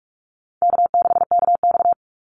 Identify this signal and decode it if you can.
Morse code, heard as a keyed tone, about 44 wpm, 700 Hz. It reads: K6KX
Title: Call Signs Repeated 2x - 45wpm